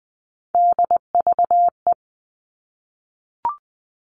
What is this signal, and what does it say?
Morse code, heard as a keyed tone, about 20 wpm, 700 Hz. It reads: DVE
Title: All Cut Numbers - 3-Digit 20wpm